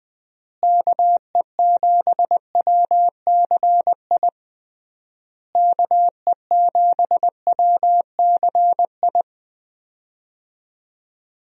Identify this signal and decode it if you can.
Morse code, heard as a keyed tone, about 20 wpm, 700 Hz. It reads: KE7WCI KE7WCI
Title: USA Call Signs - 6 Characters - Repeated 3x - 20wpm